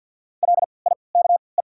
Morse code, heard as a keyed tone, about 48 wpm, spaced as for 28 wpm, 700 Hz. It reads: LIKE
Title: 3x Character Spacing - Top 100 Words - 50wpm